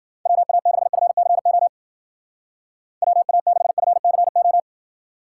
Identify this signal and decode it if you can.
Morse code, heard as a keyed tone, about 49 wpm, 700 Hz. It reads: WA6FCO WA6FCO
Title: Speed Racing - USA 2x3 Call Signs 40wpm